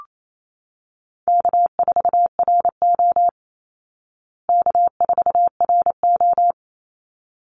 Morse code, heard as a keyed tone, about 28 wpm, 700 Hz. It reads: K4RO K4RO
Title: Call Signs Repeated 3x - 28wpm